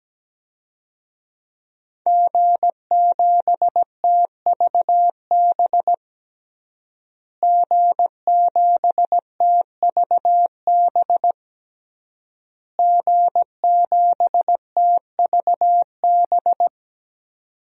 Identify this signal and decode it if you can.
Morse code, heard as a keyed tone, about 17 wpm, 700 Hz. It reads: G7TVB G7TVB G7TVB
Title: DXCC-100 Call Signs - 5 Characters - Repeated 3x - 17wpm